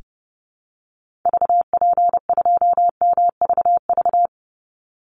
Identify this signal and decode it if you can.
Morse code, heard as a keyed tone, about 30 wpm, 700 Hz. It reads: VP2MVV
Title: Call Signs - 30wpm